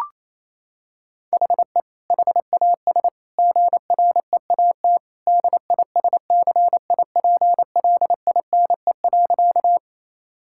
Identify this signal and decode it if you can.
Morse code, heard as a keyed tone, about 27 wpm, 700 Hz. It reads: HE HAS GREAT DISCIPLINE.
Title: Sentences from Top 1000 Words - Review Part 6 - 28wpm